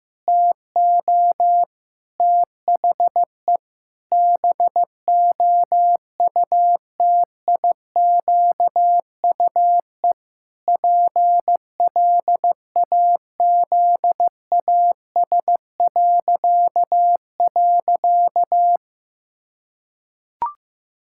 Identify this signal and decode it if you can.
Morse code, heard as a keyed tone, about 15 wpm, 700 Hz. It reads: TO THE BOUTIQUE PLAZAS..
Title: Pangram Challenge v2 - 15wpm